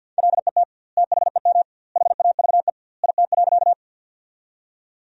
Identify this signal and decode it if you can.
Morse code, heard as a keyed tone, about 52 wpm, 700 Hz. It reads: LET THEM HAVE IT.
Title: 2x Word-Spacing - Sentences from Top 300 Words - 50wpm